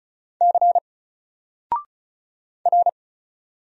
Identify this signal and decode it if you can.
Morse code, heard as a keyed tone, about 36 wpm, 700 Hz. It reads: C R
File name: Single-Letter-Number-35wpm